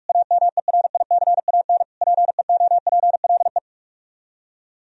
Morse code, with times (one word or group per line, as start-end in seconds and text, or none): 0.1-1.8 AMERICAN
2.0-3.6 PEOPLE